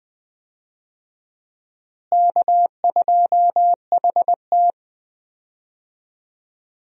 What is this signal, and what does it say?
Morse code, heard as a keyed tone, about 21 wpm, 700 Hz. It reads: K2HT
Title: USA Call Signs - 4 Characters - Repeated 3x - 20wpm